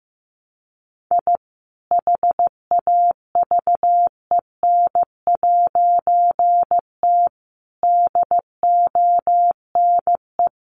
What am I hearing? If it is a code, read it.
Morse code, heard as a keyed tone, about 15 wpm, 700 Hz. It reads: I HAVEN'T DONE
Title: A Sound Of Thunder 15wpm